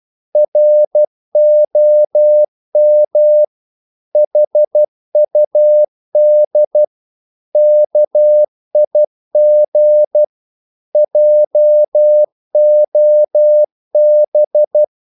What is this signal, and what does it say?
Morse code, heard as a keyed tone, about 12 wpm, 600 Hz. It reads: ROM HUD KIG JOB
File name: Korte ord DK 12wpm